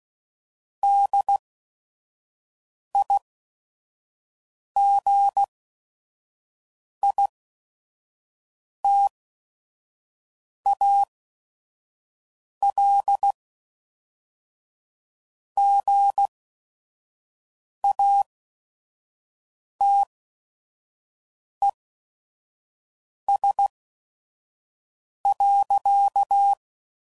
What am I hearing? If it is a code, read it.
Morse code, heard as a keyed tone, about 16 wpm, spaced as for 5 wpm, 800 Hz. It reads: DIGITAL GATES.
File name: CW Code Groups @5WPM / Technical Subjects / Gates Intro